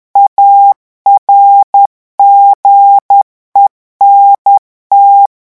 Morse code, heard as a keyed tone, about 10 wpm, 800 Hz. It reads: ARGENT
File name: revision9 citations humoristiques vitesse 10 mots minute